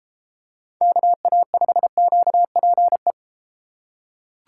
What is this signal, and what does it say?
Morse code, heard as a keyed tone, about 32 wpm, 700 Hz. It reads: KA5QPE